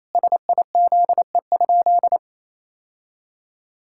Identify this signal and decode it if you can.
Morse code, heard as a keyed tone, about 28 wpm, 700 Hz. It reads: SIZE?
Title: Sentences from Top 1000 Words - Review Part 6 - 28wpm